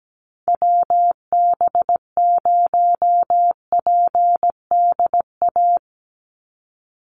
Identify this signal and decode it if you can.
Morse code, heard as a keyed tone, about 17 wpm, 700 Hz. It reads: WB0PDA
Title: USA Call Signs 17wpm